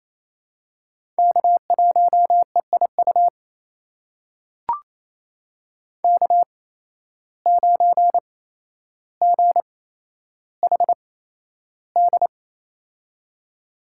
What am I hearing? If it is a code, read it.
Morse code, heard as a keyed tone, about 28 wpm, spaced as for 12 wpm, 700 Hz. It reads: K1EIU K9GHD
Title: USA Call Signs - 5 Characters - 8x Character Spacing - 28wpm